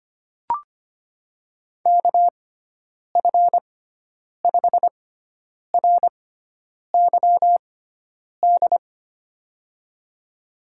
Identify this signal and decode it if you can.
Morse code, heard as a keyed tone, about 25 wpm, spaced as for 9 wpm, 700 Hz. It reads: KF5RYD
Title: USA Call Signs - 6x Character Spacing - 25wpm